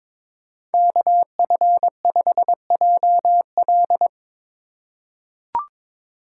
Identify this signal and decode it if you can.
Morse code, heard as a keyed tone, about 22 wpm, 700 Hz. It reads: KF5JL
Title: USA Call Signs - 5 Characters - 3x Character Spacing - 22wpm